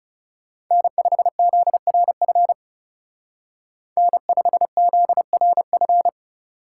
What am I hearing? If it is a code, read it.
Morse code, heard as a keyed tone, about 36 wpm, 700 Hz. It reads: N5ZRF N5ZRF
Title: Speed Racing - USA 1x3 Call Signs 30wpm